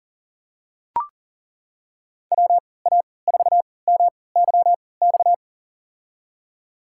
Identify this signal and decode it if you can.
Morse code, heard as a keyed tone, about 40 wpm, spaced as for 23 wpm, 700 Hz. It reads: WA4MYX
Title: USA Call Signs - 3x Character Spacing - 40wpm